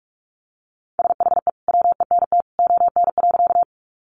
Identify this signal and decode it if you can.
Morse code, heard as a keyed tone, about 44 wpm, 700 Hz. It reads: SHE WENT ON.
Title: Sentences from Top 200 Words 45wpm